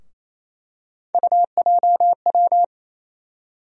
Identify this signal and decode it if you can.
Morse code, heard as a keyed tone, about 28 wpm, 700 Hz. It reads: UJW
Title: USA Call Sign Postfixes - 3 Letters 28wpm